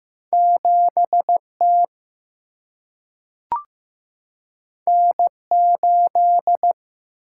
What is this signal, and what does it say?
Morse code, heard as a keyed tone, about 15 wpm, 700 Hz. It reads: 7 T N 8
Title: Single-Letter Single-Number 15wpm